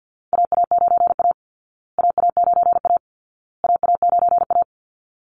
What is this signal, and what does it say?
Morse code, heard as a keyed tone, about 50 wpm, 700 Hz. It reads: AA9A AA9A AA9A
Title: Call Signs Repeated 2x - 50wpm